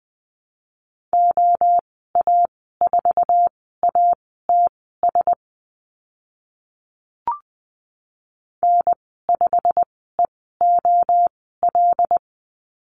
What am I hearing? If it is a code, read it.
Morse code, heard as a keyed tone, about 20 wpm, spaced as for 14 wpm, 700 Hz. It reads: OA4ATS N5EOL
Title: 088 Lesson 15g - Number 5 - Call Signs 20wpm